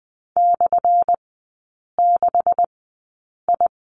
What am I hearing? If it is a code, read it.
Morse code, heard as a keyed tone, about 20 wpm, spaced as for 8 wpm, 700 Hz. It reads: /6I
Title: 178 Lesson 29e - Slash - Three Characters 20wpm